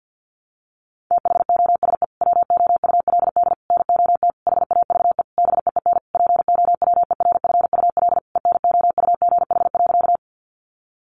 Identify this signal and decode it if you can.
Morse code, heard as a keyed tone, about 50 wpm, 700 Hz. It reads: THOSE WOULD NOT HAVE BEEN POWERFUL ENOUGH.
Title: Sentences from Top 1300 Words 50wpm